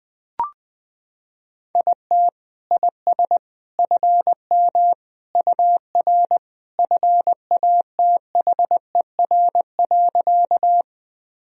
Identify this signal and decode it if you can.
Morse code, heard as a keyed tone, about 21 wpm, 700 Hz. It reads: IT IS FM UR FATHER.
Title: Sentences from Top 300 Words and Abbreviations 20wpm